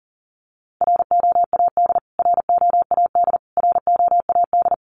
Morse code, heard as a keyed tone, about 40 wpm, 700 Hz. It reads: ROAD ROAD ROAD
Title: Top 500 Words Repeated 40wpm